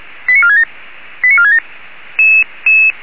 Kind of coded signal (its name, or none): five-tone selective call